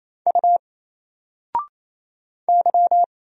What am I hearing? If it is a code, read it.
Morse code, heard as a keyed tone, about 28 wpm, 700 Hz. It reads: U Y
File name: Single-Letter-Number-28wpm